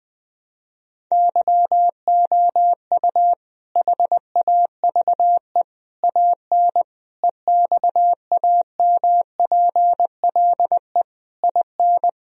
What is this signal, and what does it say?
Morse code, heard as a keyed tone, about 20 wpm, 700 Hz. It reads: YOU HAVE AN EXAMPLE IN
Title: Sentences from Top 1000 Words - Review Part 1 - 20wpm